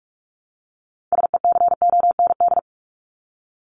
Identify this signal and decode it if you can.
Morse code, heard as a keyed tone, about 44 wpm, 700 Hz. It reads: SECOND
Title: Top 500 Words Repeated 45wpm